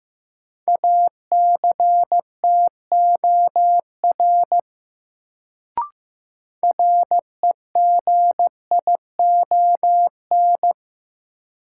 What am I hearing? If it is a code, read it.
Morse code, heard as a keyed tone, about 15 wpm, 700 Hz. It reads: ACTOR REGION
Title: Sentences from Top 1500 Words - New Word Drill - 15wpm